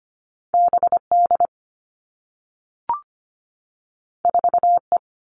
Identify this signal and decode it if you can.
Morse code, heard as a keyed tone, about 25 wpm, 700 Hz. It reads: BD 4E
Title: All Cut Numbers - 2-Digit 25wpm